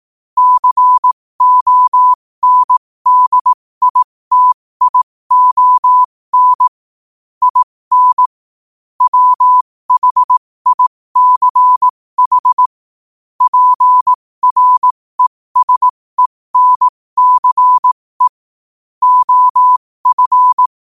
Morse code, heard as a keyed tone, about 18 wpm, spaced as for 15 wpm, 1000 Hz. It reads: CONDITION IN WHICH PRESENCE OF
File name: qotdfiles / 15wpm